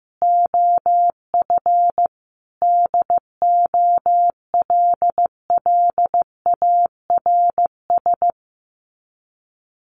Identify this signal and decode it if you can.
Morse code, heard as a keyed tone, about 15 wpm, 700 Hz. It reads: OF DOLLARS
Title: Sets of 5 Words Encore2 15wpm